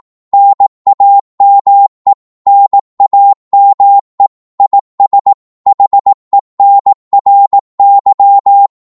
Morse code, heard as a keyed tone, about 18 wpm, 800 Hz. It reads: NAME NAME IS HENRY
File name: Morse / qso-18wpm